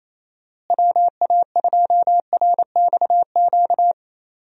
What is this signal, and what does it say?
Morse code, heard as a keyed tone, about 28 wpm, 700 Hz. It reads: WA2RXQ
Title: USA Call Signs - 6 Characters - Repeated 2x - 28wpm